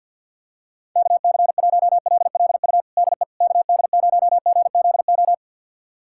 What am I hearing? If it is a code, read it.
Morse code, heard as a keyed tone, about 52 wpm, 700 Hz. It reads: KC1LLU DE KD0GZO
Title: USA Callsign DE Callsign 50wpm